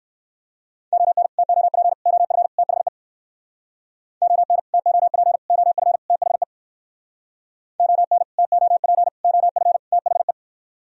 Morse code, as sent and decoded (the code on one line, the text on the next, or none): --- -. / - --- .--. / --- ..-. / - .... / --- -. / - --- .--. / --- ..-. / - .... / --- -. / - --- .--. / --- ..-. / - .... .
ON TOP OF THE ON TOP OF THE ON TOP OF THE